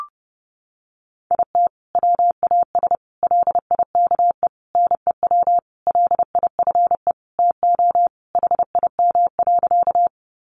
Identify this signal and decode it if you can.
Morse code, heard as a keyed tone, about 28 wpm, 700 Hz. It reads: IT WAS LIKE NEW LIFE TO HIM.
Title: Sentences from Top 300 Words 30wpm